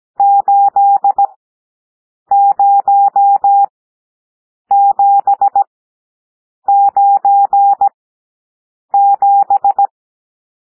Morse code, heard as a keyed tone, about 17 wpm, spaced as for 7 wpm, 800 Hz. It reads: 80797
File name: SOUND NUMBERS 17 wpm